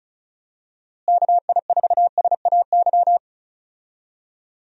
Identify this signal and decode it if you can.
Morse code, heard as a keyed tone, about 36 wpm, 700 Hz. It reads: KI4SAY
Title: USA Call Signs - 6 Characters - Repeated 2x - 35wpm